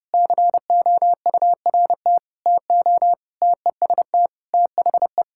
Morse code, heard as a keyed tone, about 30 wpm, 700 Hz. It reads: COURT TO TEST THE
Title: Sentences from Top 1600 Words 30wpm